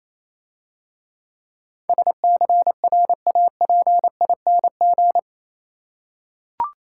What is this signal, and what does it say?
Morse code, heard as a keyed tone, about 27 wpm, 690 Hz. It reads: SCRAPING
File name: Easy Words - Speed-Racing with Hints - 22wpm